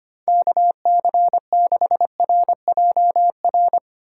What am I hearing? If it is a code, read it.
Morse code, heard as a keyed tone, about 29 wpm, 700 Hz. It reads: KC6RJR
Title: Speed Racing - USA 2x3 Call Signs 25wpm